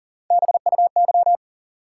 Morse code, heard as a keyed tone, about 40 wpm, 700 Hz. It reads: BUY
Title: Sentences from Top 700 Words - New Word Drill - 40wpm